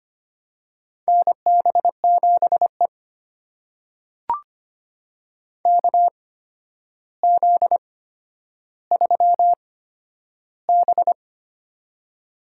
Morse code, heard as a keyed tone, about 25 wpm, spaced as for 11 wpm, 700 Hz. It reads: NB7E KZ3B
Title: USA Call Signs - 4 Characters - 8x Character Spacing - 25wpm